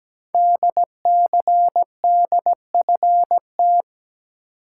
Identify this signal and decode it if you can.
Morse code, heard as a keyed tone, about 17 wpm, 700 Hz. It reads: DCDFT
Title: Code Groups - 5 Letters - 17wpm